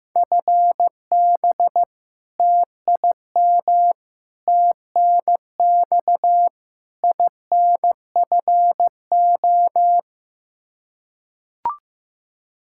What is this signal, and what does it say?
Morse code, heard as a keyed tone, about 15 wpm, 700 Hz. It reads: FB TIM TNX INFO
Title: K1USN SST QSO Practice 15wpm